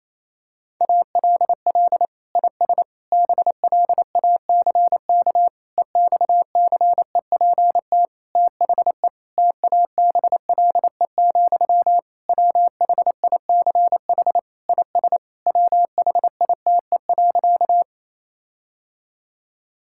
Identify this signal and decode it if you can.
Morse code, heard as a keyed tone, about 27 wpm, 700 Hz. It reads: ALL IS BLACK EXCEPT THE TABLE, WHICH IS WHITE.
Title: Sentences from Top 1600 Words 28wpm